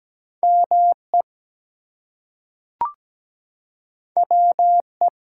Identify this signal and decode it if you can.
Morse code, heard as a keyed tone, about 17 wpm, 700 Hz. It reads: ME WE
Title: Two Letter Words - 17wpm